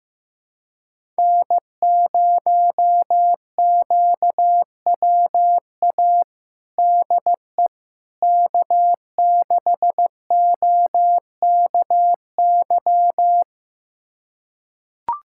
Morse code, heard as a keyed tone, about 15 wpm, 700 Hz. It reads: N0QWA DE K6OKY
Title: USA Callsign DE Callsign - 5 Characters - 15wpm